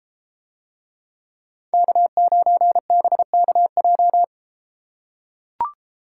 Morse code, heard as a keyed tone, about 33 wpm, 700 Hz. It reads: K9BKJ